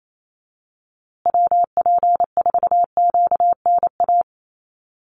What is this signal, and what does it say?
Morse code, heard as a keyed tone, about 28 wpm, 700 Hz. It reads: WP4QNA
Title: USA Call Signs Repeated 3x 28wpm